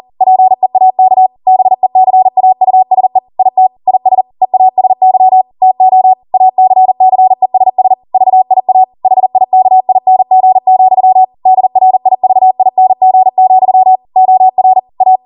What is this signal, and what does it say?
Morse code, heard as a keyed tone, about 40 wpm, 750 Hz. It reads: PEAK BECAUSE IT IS EASY TO ACCESS VIA HIKING, DRIVING, OR A